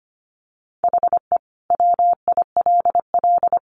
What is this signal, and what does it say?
Morse code, heard as a keyed tone, about 25 wpm, 700 Hz. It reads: HE WILL
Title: Sentences from Top 1500 Words 25wpm